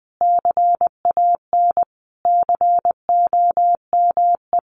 Morse code, heard as a keyed tone, about 20 wpm, 700 Hz. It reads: CAN COME
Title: A Sound Of Thunder 20wpm